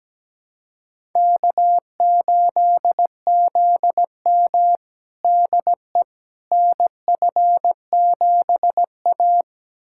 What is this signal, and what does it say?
Morse code, heard as a keyed tone, about 17 wpm, 700 Hz. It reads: K8ZM DE NF7A
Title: USA Callsign DE Callsign - 4 Characters - 17wpm